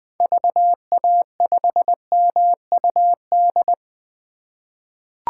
Morse code, heard as a keyed tone, about 20 wpm, 700 Hz. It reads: VA5MUD
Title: 2023 POTA Activator Call Signs - 20wpm